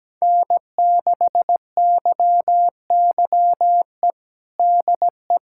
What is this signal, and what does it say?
Morse code, heard as a keyed tone, about 17 wpm, 700 Hz. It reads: N6YYE DE
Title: USA Callsign DE Callsign - 5 Characters - 17wpm